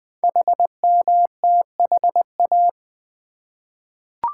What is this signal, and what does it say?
Morse code, heard as a keyed tone, about 20 wpm, 700 Hz. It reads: HMTHA
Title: Code Groups - 5 Letters - 20wpm